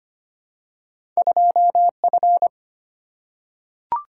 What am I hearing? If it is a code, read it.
Morse code, heard as a keyed tone, about 25 wpm, 700 Hz. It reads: 2F
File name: Two Letter-Number Combinations - 25wpm